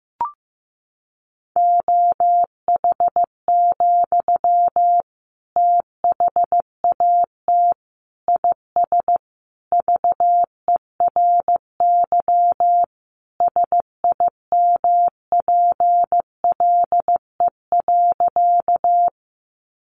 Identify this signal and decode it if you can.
Morse code, heard as a keyed tone, about 15 wpm, 700 Hz. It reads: OH, THAT IS VERY SIMPLE.
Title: Sentences from Top 500 Words 15wpm